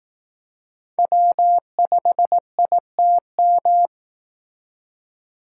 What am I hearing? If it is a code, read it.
Morse code, heard as a keyed tone, about 19 wpm, 700 Hz. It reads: W5ITM